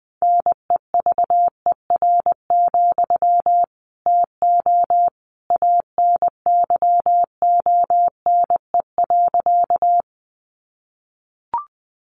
Morse code, heard as a keyed tone, about 20 wpm, 700 Hz. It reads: NEVER, TO ANYONE.
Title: Sentences from Top 900 Words 20wpm